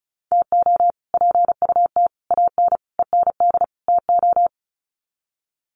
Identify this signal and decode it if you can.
Morse code, heard as a keyed tone, about 34 wpm, 700 Hz. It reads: TO PUT AN END TO
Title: Sets of 5 Words Encore2 35wpm